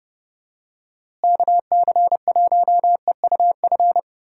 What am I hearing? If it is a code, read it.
Morse code, heard as a keyed tone, about 32 wpm, 700 Hz. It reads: KC1EUF
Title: USA Call Signs - 6 Characters - Repeated 2x - 30wpm